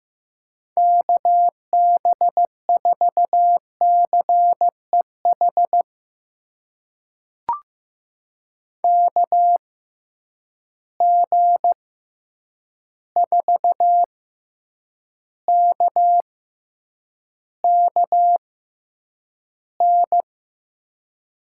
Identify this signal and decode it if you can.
Morse code, heard as a keyed tone, about 15 wpm, spaced as for 8 wpm, 700 Hz. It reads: KB4CEH KG4KKN
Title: USA Call Signs - 6 Characters - 6x Character Spacing - 15wpm